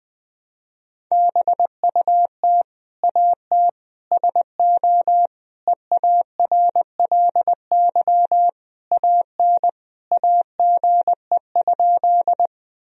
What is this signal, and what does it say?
Morse code, heard as a keyed tone, about 20 wpm, 700 Hz. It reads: BUT AT SO EARLY AN AGE?
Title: Sentences from Top 1000 Words - Review Part 3 - 20wpm